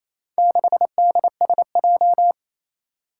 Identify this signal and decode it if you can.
Morse code, heard as a keyed tone, about 28 wpm, 700 Hz. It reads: 6DSJ
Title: Four Challenging Letter-Number Combinations 28wpm